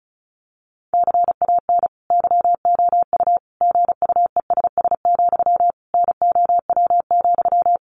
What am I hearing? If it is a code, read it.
Morse code, heard as a keyed tone, about 34 wpm, 700 Hz. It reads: CAN YOU GUESS, NOW,
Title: Sentences from Top 1500 Words 35wpm